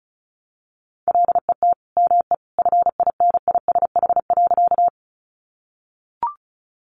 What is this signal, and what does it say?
Morse code, heard as a keyed tone, about 34 wpm, 700 Hz. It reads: LET ME FINISH.
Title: Sentences from Top 1200 Words 35wpm